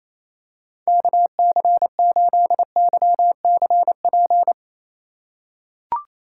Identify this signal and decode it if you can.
Morse code, heard as a keyed tone, about 28 wpm, 700 Hz. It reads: KC8YCP
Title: USA Call Signs - 6 Characters - 3x Character Spacing - 28wpm